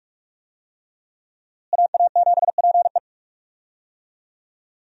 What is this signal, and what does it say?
Morse code, heard as a keyed tone, about 46 wpm, 700 Hz. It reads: AA7PE
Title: USA Call Signs - 5 Characters - Repeated 3x - 45wpm